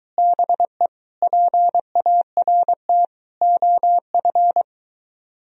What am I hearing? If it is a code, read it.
Morse code, heard as a keyed tone, about 24 wpm, 700 Hz. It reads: BE PART OF